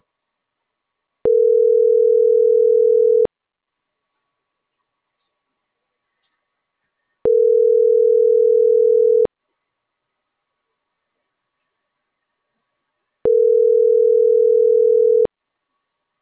Ringback tone (North America), 440 + 480 Hz.